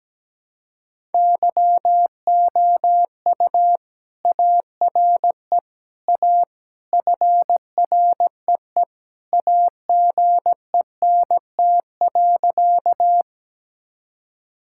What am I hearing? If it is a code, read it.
Morse code, heard as a keyed tone, about 17 wpm, 700 Hz. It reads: YOU ARE A FREE AGENT.
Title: Sentences from Top 1600 Words 17wpm